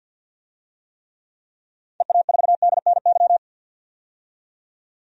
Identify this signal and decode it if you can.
Morse code, heard as a keyed tone, about 50 wpm, 700 Hz. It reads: EA4DNY
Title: DXCC-100 Call Signs - 6 Characters - 50wpm